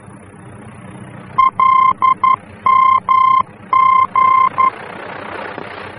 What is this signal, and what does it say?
Morse code, heard as a keyed tone, about 11 wpm, 1050 Hz. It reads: LMG
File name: france / mx / LMG-F-VOR